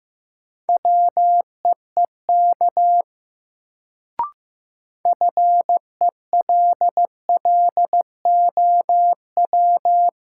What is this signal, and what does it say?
Morse code, heard as a keyed tone, about 15 wpm, 700 Hz. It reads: WEEK FELLOW
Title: Sets of 5 Words Encore - New Word Drill - 15wpm